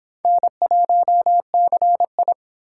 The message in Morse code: N1CI

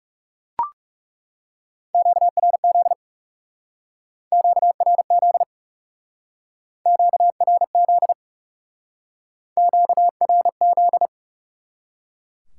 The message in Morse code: QRZ QRZ QRZ QRZ

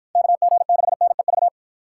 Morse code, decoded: KG6NEV

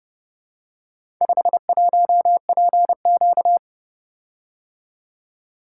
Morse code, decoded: 51PQ

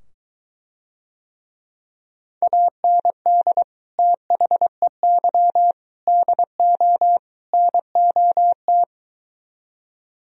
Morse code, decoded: AND THEY DO NOT

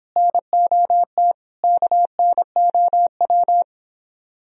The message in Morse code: NOT KNOW

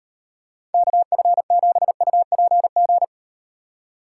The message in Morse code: KF7UPG

